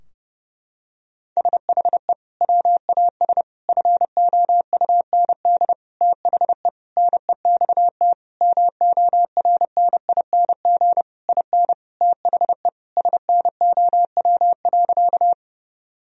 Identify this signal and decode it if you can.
Morse code, heard as a keyed tone, about 30 wpm, 700 Hz. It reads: SHE WAS FOUND THE NEXT MORNING IN THE SNOW.
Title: Sentences from Top 1000 Words - Review Part 3 - 30wpm